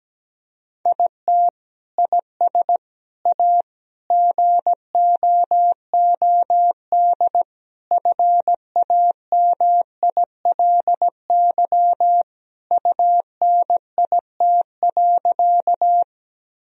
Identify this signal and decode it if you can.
Morse code, heard as a keyed tone, about 17 wpm, 700 Hz. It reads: IT IS A GOOD FAMILY UNIT.
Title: Sentences from Top 1000 Words - Review Part 2 - 17wpm